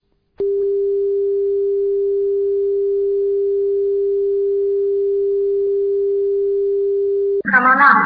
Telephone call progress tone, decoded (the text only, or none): dial tone